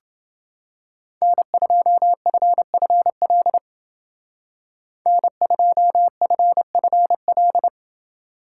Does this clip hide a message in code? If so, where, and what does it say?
Yes: Morse code from 1.2 to 7.7 s, heard as a keyed tone, about 28 wpm, 700 Hz. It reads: N2FFL N2FFL